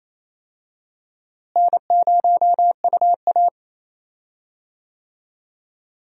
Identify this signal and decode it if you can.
Morse code, heard as a keyed tone, about 28 wpm, 700 Hz. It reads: N0UA
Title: USA Call Signs - 4 Characters - Repeated 3x - 28wpm